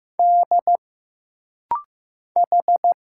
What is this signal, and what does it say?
Morse code, heard as a keyed tone, about 15 wpm, 700 Hz. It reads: D H
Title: Single Letters - Rapid-Fire 15wpm